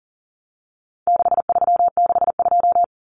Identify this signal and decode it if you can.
Morse code, heard as a keyed tone, about 40 wpm, 700 Hz. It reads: 6362
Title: Numbers - 4-Digit 40wpm